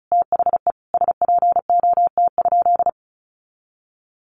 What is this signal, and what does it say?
Morse code, heard as a keyed tone, about 34 wpm, 700 Hz. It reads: THE SPOT?
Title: Sentences from Top 1100 Words 35wpm